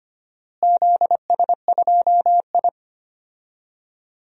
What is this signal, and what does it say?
Morse code, heard as a keyed tone, about 25 wpm, 700 Hz. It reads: ZS2I
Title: DXCC-100 Call Signs - 4 Characters - Repeated 2x - 25wpm